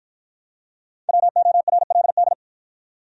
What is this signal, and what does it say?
Morse code, heard as a keyed tone, about 50 wpm, 700 Hz. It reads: WORLD